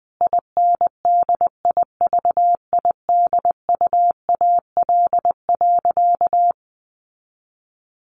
Morse code, heard as a keyed tone, about 20 wpm, 700 Hz. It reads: INDIVIDUAL.
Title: Sentences from Top 1400 Words 20wpm